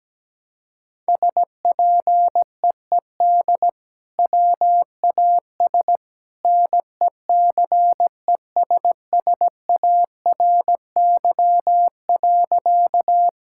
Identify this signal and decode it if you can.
Morse code, heard as a keyed tone, about 17 wpm, 700 Hz. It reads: SPEED WAS NECESSARY.
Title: Sentences from Top 1000 Words - Review Part 5 - 17wpm